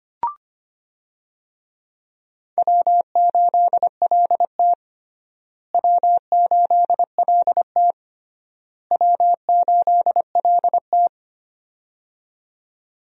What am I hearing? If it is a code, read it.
Morse code, heard as a keyed tone, about 25 wpm, 700 Hz. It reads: W8LT W8LT W8LT
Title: USA Call Signs - 4 Characters - Repeated 3x - 25wpm